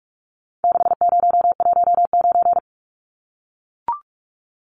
Morse code, heard as a keyed tone, about 44 wpm, 700 Hz. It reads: 6019
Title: Numbers - 4-Digit 45wpm